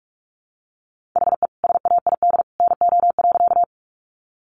Morse code, heard as a keyed tone, about 47 wpm, 700 Hz. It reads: HE SAID NO.
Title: Sentences from Top 100 Words 45wpm